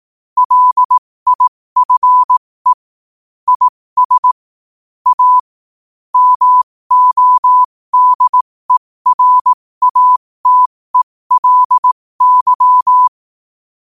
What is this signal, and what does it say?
Morse code, heard as a keyed tone, about 18 wpm, spaced as for 15 wpm, 1000 Hz. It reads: LIFE IS A MODERATELY
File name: qotdfiles / 15wpm